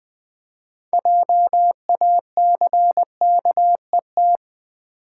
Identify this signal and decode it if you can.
Morse code, heard as a keyed tone, about 20 wpm, 700 Hz. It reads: JACKET
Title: Build A Word - 3 Letters At A Time - 6-Letter Words - 20wpm